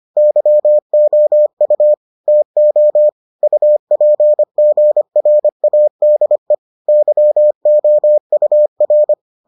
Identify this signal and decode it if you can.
Morse code, heard as a keyed tone, about 25 wpm, 600 Hz. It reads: YOU TO UPGRADE YOUR